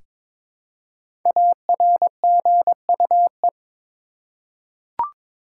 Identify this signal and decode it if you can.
Morse code, heard as a keyed tone, about 22 wpm, 700 Hz. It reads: ARGUE
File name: 5 Letter Words - First 3 Letters Given - 22wpm